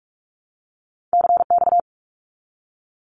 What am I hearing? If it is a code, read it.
Morse code, heard as a keyed tone, about 44 wpm, 700 Hz. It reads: CX
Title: Abbreviations 45wpm